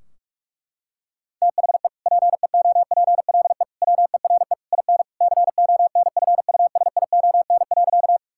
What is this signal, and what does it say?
Morse code, heard as a keyed tone, about 41 wpm, 700 Hz. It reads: THE PEOPLE WERE IN CONFUSION.
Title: Sentences from Top 1600 Words 45wpm